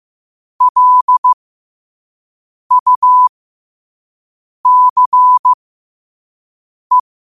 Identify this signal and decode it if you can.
Morse code, heard as a keyed tone, about 15 wpm, spaced as for 5 wpm, 1000 Hz. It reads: LUCE
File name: qotdfiles / 5wpm